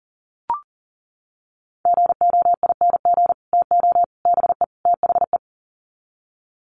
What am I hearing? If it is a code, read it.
Morse code, heard as a keyed tone, about 40 wpm, 700 Hz. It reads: GOING TO BE THE
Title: Sets of 4 Words 40wpm